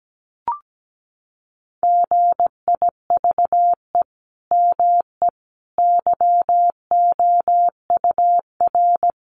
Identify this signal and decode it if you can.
Morse code, heard as a keyed tone, about 17 wpm, 700 Hz. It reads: GIVE ME YOUR
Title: Sentences from Top 1100 Words 17wpm